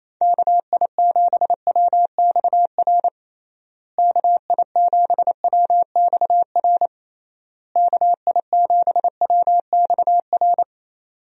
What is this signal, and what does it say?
Morse code, heard as a keyed tone, about 28 wpm, 700 Hz. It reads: KI7WXR KI7WXR KI7WXR
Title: USA Call Signs - 6 Characters - Repeated 3x - 28wpm